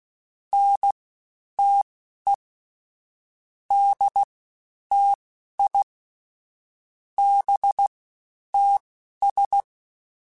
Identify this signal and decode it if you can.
Morse code, heard as a keyed tone, about 16 wpm, spaced as for 10 wpm, 800 Hz. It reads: NTE DTI BTS
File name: CW Comparisons / CW Comparisons-mp3s / CW Comparisons Broken Characters @15